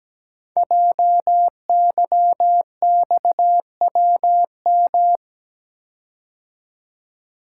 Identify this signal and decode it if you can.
Morse code, heard as a keyed tone, about 17 wpm, 700 Hz. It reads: JYXWM
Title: Code Groups - 5 Letters - 17wpm